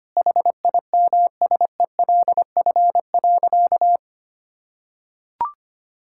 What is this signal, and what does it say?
Morse code, heard as a keyed tone, about 25 wpm, 700 Hz. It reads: HIMSELF.
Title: Sentences from Top 1600 Words 25wpm